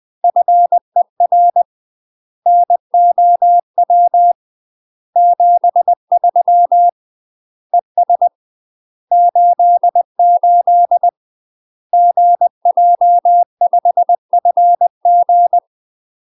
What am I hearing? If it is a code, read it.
Morse code, heard as a keyed tone, about 20 wpm, 700 Hz. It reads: FER NOW 73 ES 88 GJ5FG
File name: UK MASTER QSOs TRAINING/20WPM